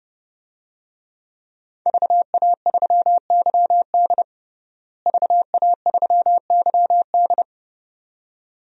Morse code, heard as a keyed tone, about 31 wpm, 700 Hz. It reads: VA3YD VA3YD
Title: DXCC-100 Call Signs - 5 Characters - Repeated 2x - 30wpm